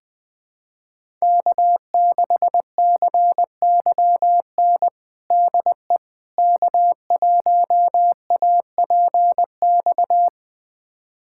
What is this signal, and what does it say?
Morse code, heard as a keyed tone, about 20 wpm, 700 Hz. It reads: K6CYN DE K1APX
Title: USA Callsign DE Callsign - 5 Characters - 20wpm